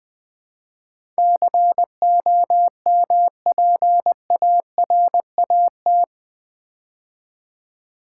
Morse code, heard as a keyed tone, about 19 wpm, 700 Hz. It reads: COMPARAT
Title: Build A Word - 4 Letters At A Time - 11-Letter Words - 20wpm